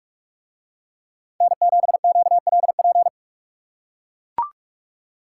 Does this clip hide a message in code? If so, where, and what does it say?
Yes: Morse code from 1.4 to 3.1 s, heard as a keyed tone, about 44 wpm, 700 Hz. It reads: N7QLP